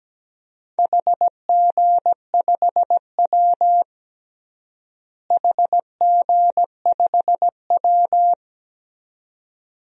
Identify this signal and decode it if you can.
Morse code, heard as a keyed tone, about 17 wpm, 700 Hz. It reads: HG5W HG5W
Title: DXCC-100 Call Signs - 4 Characters - Repeated 2x - 17wpm